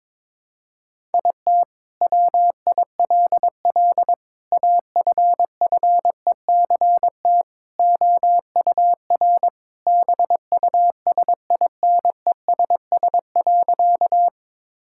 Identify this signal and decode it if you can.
Morse code, heard as a keyed tone, about 22 wpm, 700 Hz. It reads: IT WILL AFFECT OUR BUSINESS.
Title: Sentences from Top 1000 Words - Review Part 6 - 22wpm